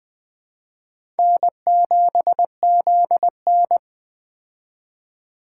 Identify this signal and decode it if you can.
Morse code, heard as a keyed tone, about 20 wpm, 700 Hz. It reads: N7ZN
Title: K1USN SST QSO Practice - 20wpm